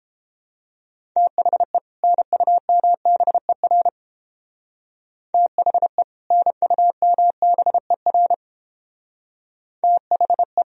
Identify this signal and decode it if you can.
Morse code, heard as a keyed tone, about 31 wpm, 700 Hz. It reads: THE NUMBER THE NUMBER THE